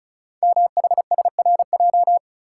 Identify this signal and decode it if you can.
Morse code, heard as a keyed tone, about 36 wpm, 690 Hz. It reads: MHSRJ
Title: Code Groups - 5 Letters - 35wpm